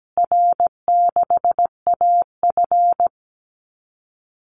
Morse code, heard as a keyed tone, about 17 wpm, 700 Hz. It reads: R6AF
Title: Call Signs Repeated 2x - 17wpm